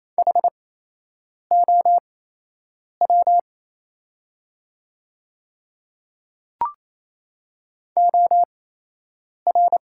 Morse code, heard as a keyed tone, about 28 wpm, spaced as for 8 wpm, 700 Hz. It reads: HOW OR
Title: 8x Character Spacing - Top 100 Words - 28wpm